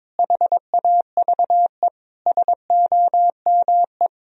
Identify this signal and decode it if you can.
Morse code, heard as a keyed tone, about 22 wpm, 700 Hz. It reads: HAVE SOME
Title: Sentences from Top 1000 Words - Review Part 1 - 22wpm